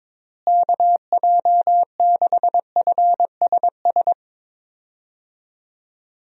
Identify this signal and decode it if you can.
Morse code, heard as a keyed tone, about 22 wpm, 700 Hz. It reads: KJ6FSS
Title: USA Call Signs - 6 Characters - Repeated 2x - 22wpm